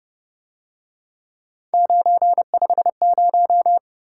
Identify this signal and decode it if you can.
Morse code, heard as a keyed tone, about 31 wpm, 700 Hz. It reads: 950